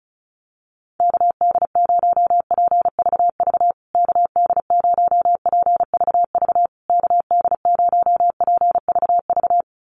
Morse code, heard as a keyed tone, about 34 wpm, 700 Hz. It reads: KD0PVV KD0PVV KD0PVV
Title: USA Call Signs Repeated 3x 35wpm